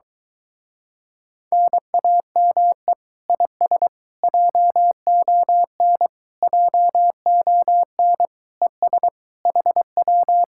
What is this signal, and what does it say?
Morse code, heard as a keyed tone, about 23 wpm, 700 Hz. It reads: NAME IS JON JON ES HW